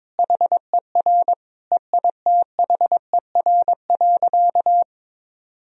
Morse code, heard as a keyed tone, about 22 wpm, 700 Hz. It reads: HER EITHER.
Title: Sentences from Top 1000 Words - Review Part 4 - 22wpm